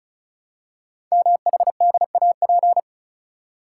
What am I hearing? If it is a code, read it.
Morse code, heard as a keyed tone, about 36 wpm, 690 Hz. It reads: MHDAP
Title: Code Groups - 5 Letters - 35wpm